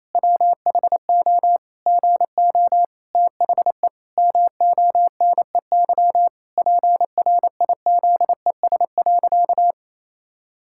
Morse code, heard as a keyed tone, about 28 wpm, 700 Hz. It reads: WHO GO THE MONEY PRIZES.
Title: Pangram Challenge v2 - 28wpm